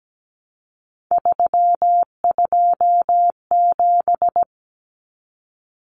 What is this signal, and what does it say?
Morse code, heard as a keyed tone, about 17 wpm, 700 Hz. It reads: 327
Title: Common Cut Numbers - 3-Digit 17wpm